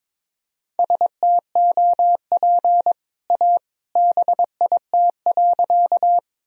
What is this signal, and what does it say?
Morse code, heard as a keyed tone, about 22 wpm, 700 Hz. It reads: STOP A BIT.
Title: Sentences from Top 1000 Words - Review Part 1 - 22wpm